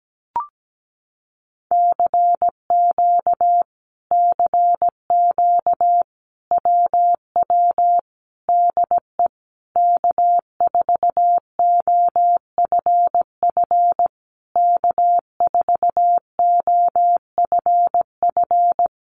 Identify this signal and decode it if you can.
Morse code, heard as a keyed tone, about 17 wpm, 700 Hz. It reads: CQ CQ WW DE K4OFF K4OFF
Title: Common QSO Phrases Encore 17wpm